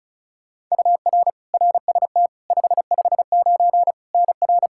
Morse code, heard as a keyed tone, about 36 wpm, 700 Hz. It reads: UR RST 559 NR
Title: Practice QSO - Exchange 1 - Beginning - 35wpm